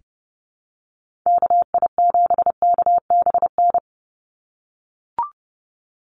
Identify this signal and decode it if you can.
Morse code, heard as a keyed tone, about 30 wpm, 700 Hz. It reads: KI7KBN
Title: USA Call Signs 30wpm